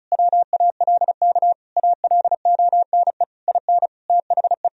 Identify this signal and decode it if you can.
Morse code, heard as a keyed tone, about 36 wpm, 700 Hz. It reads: WALK ALONE IN THE
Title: Sentences from Top 1000 Words - Review Part 2 - 35wpm